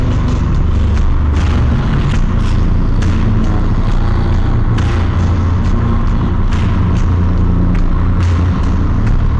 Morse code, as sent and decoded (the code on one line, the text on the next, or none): none